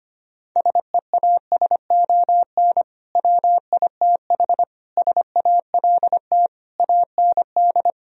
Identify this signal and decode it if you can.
Morse code, heard as a keyed tone, about 24 wpm, 700 Hz. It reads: SEASON WITH SALT AND
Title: Speed-Racing - Sets of 5 Words 25wpm